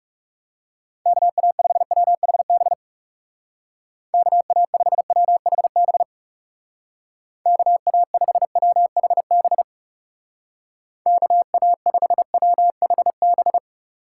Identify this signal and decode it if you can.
Morse code, heard as a keyed tone, about 38 wpm, 700 Hz. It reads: KA5WHB KA5WHB KA5WHB KA5WHB